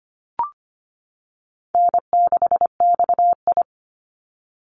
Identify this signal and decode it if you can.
Morse code, heard as a keyed tone, about 25 wpm, 700 Hz. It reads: N6XI
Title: Easy Call Signs - 25wpm